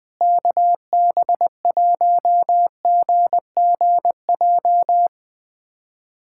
Morse code, heard as a keyed tone, about 20 wpm, 700 Hz. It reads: KB1GGJ
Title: USA Callsign DE Callsign - 6 Characters - 20wpm